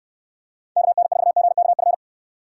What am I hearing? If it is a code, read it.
Morse code, heard as a keyed tone, about 69 wpm, 700 Hz. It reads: KN4GKF